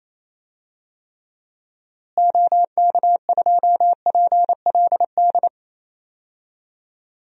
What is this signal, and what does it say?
Morse code, heard as a keyed tone, about 27 wpm, 700 Hz. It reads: OK2PLD
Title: DXCC-100 Call Signs - 6 Characters - Repeated 3x - 28wpm